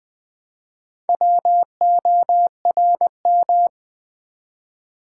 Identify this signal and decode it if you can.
Morse code, heard as a keyed tone, about 20 wpm, 700 Hz. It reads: WORM
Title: Build A Word - 1 Letter At A Time - 3 to 5-Letter Words - 20wpm